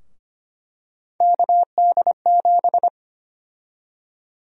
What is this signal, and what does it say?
Morse code, heard as a keyed tone, about 25 wpm, 700 Hz. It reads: KD7
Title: USA Call Sign Prefixes - 25wpm